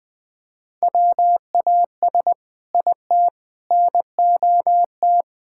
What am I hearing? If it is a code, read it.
Morse code, heard as a keyed tone, about 20 wpm, 700 Hz. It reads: WAS IT NOT
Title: Sentences from Top 1600 Words 20wpm